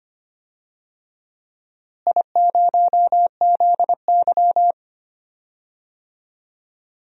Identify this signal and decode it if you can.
Morse code, heard as a keyed tone, about 25 wpm, 700 Hz. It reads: I0ZY
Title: DXCC-100 Call Signs - 4 Characters - Repeated 2x - 25wpm